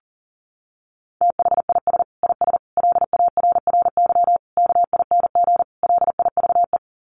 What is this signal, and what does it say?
Morse code, heard as a keyed tone, about 40 wpm, 700 Hz. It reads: THIS IS LARRY KING LIVE
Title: Sets of 5 Words Encore2 40wpm